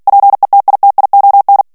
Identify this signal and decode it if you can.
Morse code, heard as a keyed tone, about 48 wpm, 800 Hz. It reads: PETITION
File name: ANGLAIS texte clair 3 vitesse 45 mots minute